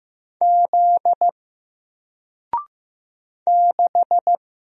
Morse code, heard as a keyed tone, about 15 wpm, 700 Hz. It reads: Z 6
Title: Single Character (Letter:Number:Punctuation:Prosign) Alt 15wpm